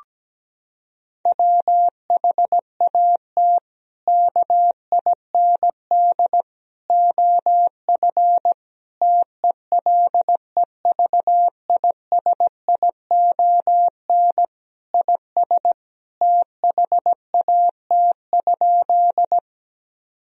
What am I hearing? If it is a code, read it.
Morse code, heard as a keyed tone, about 17 wpm, 700 Hz. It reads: WHAT KIND OF TELEVISION IS THAT?
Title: Sentences from Top 1000 Words - Review Part 4 - 17wpm